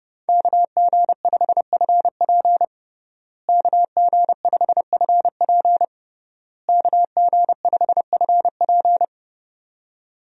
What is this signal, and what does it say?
Morse code, heard as a keyed tone, about 30 wpm, 700 Hz. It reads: KG5FP KG5FP KG5FP
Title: USA Call Signs - 5 Characters - Repeated 3x - 30wpm